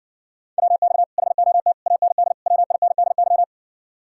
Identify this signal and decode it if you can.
Morse code, heard as a keyed tone, about 60 wpm, 700 Hz. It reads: WX HOT AND WINDY